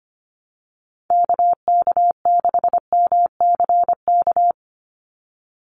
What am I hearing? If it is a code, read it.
Morse code, heard as a keyed tone, about 25 wpm, 690 Hz. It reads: KK6MCK
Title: USA Call Signs 25wpm